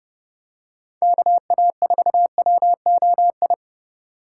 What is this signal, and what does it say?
Morse code, heard as a keyed tone, about 27 wpm, 700 Hz. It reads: KA4WOI